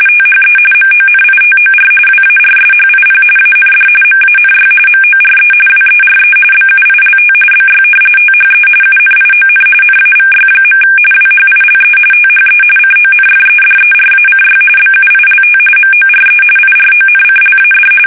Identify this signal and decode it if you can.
RTTY, about 45.45 baud. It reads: THE MONITOR WARSHIP AND MANY MORE WERE BUILT BY THE UNION NAVY. WHILE THE CONFEDERACY BUILT MORE IRONCLAD
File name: fsk850 / usnavy3